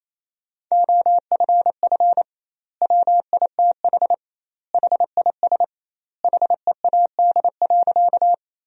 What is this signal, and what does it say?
Morse code, heard as a keyed tone, about 28 wpm, 700 Hz. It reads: OFF WITH HIS HEAD.
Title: 2x Word-Spacing - Sentences from Top 300 Words - 28wpm